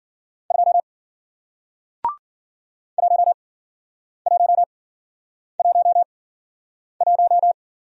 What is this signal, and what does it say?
Morse code, heard as a keyed tone, about 51 wpm, spaced as for 9 wpm, 700 Hz. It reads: 2 1111